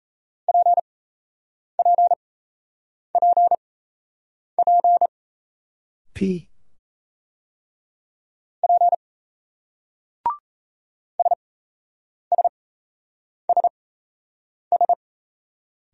Morse code, heard as a keyed tone, about 34 wpm, spaced as for 8 wpm, 700 Hz. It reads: PPPP P SSSS
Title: Speed-Racing Single Letters 28wpm